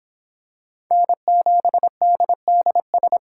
Morse code, heard as a keyed tone, about 26 wpm, 700 Hz. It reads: N7DDS